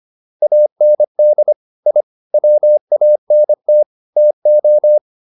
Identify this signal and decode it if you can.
Morse code, heard as a keyed tone, about 25 wpm, 600 Hz. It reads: AND I WANT TO